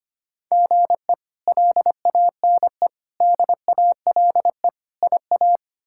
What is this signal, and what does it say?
Morse code, heard as a keyed tone, about 25 wpm, 700 Hz. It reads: GE LANE DALE IA
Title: K1USN SST QSO Practice - 25wpm